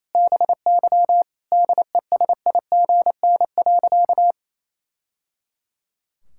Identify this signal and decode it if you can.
Morse code, heard as a keyed tone, about 27 wpm, 700 Hz. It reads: BY DESIGN.
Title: Sentences from Top 1000 Words - Review Part 5 - 28wpm